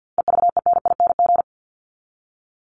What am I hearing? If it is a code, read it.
Morse code, heard as a keyed tone, about 50 wpm, 700 Hz. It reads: EVENING
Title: Top 1501-1750 Words 50wpm